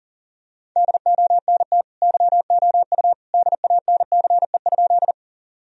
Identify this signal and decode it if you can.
Morse code, heard as a keyed tone, about 40 wpm, 700 Hz. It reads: DONT YOU DANCE?
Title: Sentences from Top 1000 Words - Review Part 4 - 40wpm